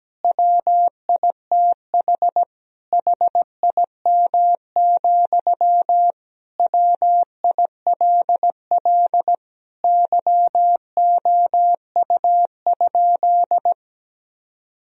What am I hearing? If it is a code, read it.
Morse code, heard as a keyed tone, about 17 wpm, 700 Hz. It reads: WITH HIM, WILL YOU?
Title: Sentences from Top 1000 Words - Review Part 1 - 17wpm